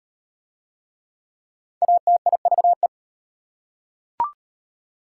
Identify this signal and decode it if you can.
Morse code, heard as a keyed tone, about 38 wpm, 700 Hz. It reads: ATIVE